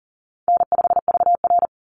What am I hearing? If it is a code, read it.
Morse code, heard as a keyed tone, about 40 wpm, 700 Hz. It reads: N5VR
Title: Easy Call Signs Repeated 3x - 40wpm